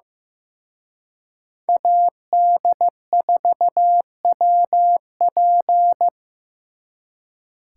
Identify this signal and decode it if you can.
Morse code, heard as a keyed tone, about 18 wpm, 700 Hz. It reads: AD4WP
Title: Speed Racing - USA 2x2 Call Signs 15wpm